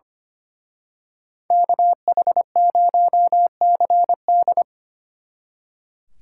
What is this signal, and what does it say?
Morse code, heard as a keyed tone, about 29 wpm, 700 Hz. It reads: KH0CD
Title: Speed Racing - USA 2x2 Call Signs 25wpm